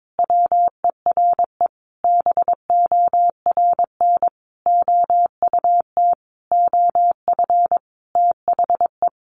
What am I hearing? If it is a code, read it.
Morse code, heard as a keyed tone, about 22 wpm, 700 Hz. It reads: WERE BORN OUT OF THE
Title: Sentences from Top 1200 Words 22wpm